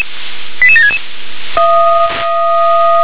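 Five-tone selective call (ZVEI), heard as quick runs of five tones, tones of about 80 ms. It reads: 88766
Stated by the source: ZVEI five-tone selective call received on SDR.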